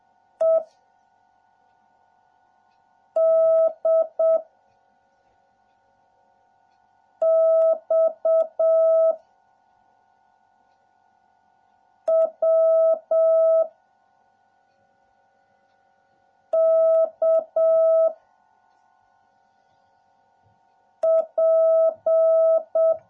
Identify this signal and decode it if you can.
Morse code, heard as a keyed tone, about 7 wpm, 660 Hz. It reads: E D X W K P